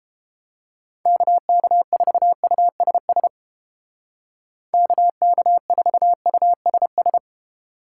Morse code, heard as a keyed tone, about 29 wpm, 700 Hz. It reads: KK4USS KK4USS